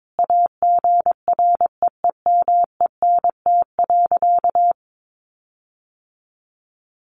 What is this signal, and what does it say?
Morse code, heard as a keyed tone, about 22 wpm, 700 Hz. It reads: AGREEMENT.
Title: Sentences from Top 1500 Words 22wpm